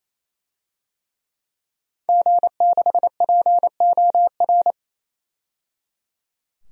Morse code, heard as a keyed tone, about 28 wpm, 700 Hz. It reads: G6POR
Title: DXCC-100 Call Signs - 5 Characters - 28wpm